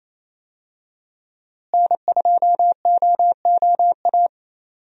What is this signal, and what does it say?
Morse code, heard as a keyed tone, about 27 wpm, 700 Hz. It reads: N2OOA
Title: USA Call Signs - 5 Characters - Repeated 2x - 28wpm